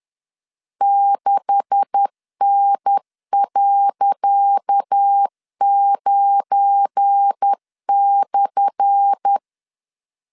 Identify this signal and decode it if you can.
Morse code, heard as a keyed tone, about 10 wpm, 800 Hz. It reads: 6N.9/
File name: LEZIONI / 10 SIGNS / 10D COMA 10wpm 3013